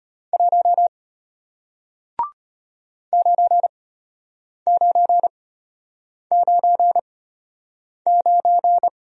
Morse code, heard as a keyed tone, about 34 wpm, spaced as for 8 wpm, 700 Hz. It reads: 1 9999